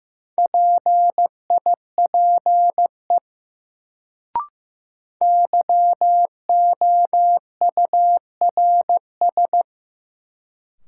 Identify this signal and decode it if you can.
Morse code, heard as a keyed tone, about 15 wpm, 700 Hz. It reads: PIPE YOURS
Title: Sentences from Top 1500 Words - New Word Drill - 15wpm